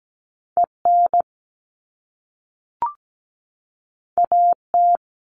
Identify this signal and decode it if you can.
Morse code, heard as a keyed tone, about 17 wpm, 700 Hz. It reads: EN AT
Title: All Cut Numbers - 2-Digit 17wpm